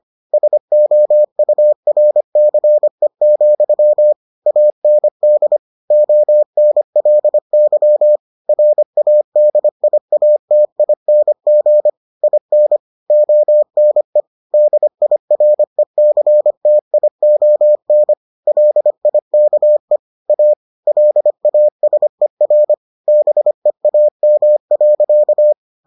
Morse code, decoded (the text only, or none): SOURCE, AND ONLY RADIATING IN ONE DIRECTION LIKE A LASER BEAM.